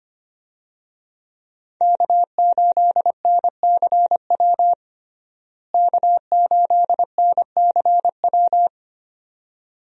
Morse code, heard as a keyed tone, about 25 wpm, 700 Hz. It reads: K8NCW K8NCW
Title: USA Call Signs - 5 Characters - Repeated 2x - 25wpm